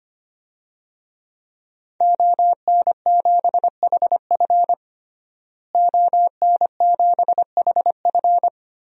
Morse code, heard as a keyed tone, about 25 wpm, 700 Hz. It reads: ON7HF ON7HF
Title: DXCC-100 Call Signs - 5 Characters - Repeated 3x - 25wpm